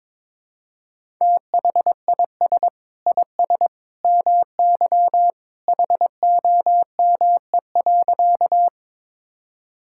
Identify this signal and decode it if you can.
Morse code, heard as a keyed tone, about 22 wpm, 700 Hz. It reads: THIS IS MY HOME.
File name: practice / 22wpm / Sentences with 2-Word Phrases - 22wpm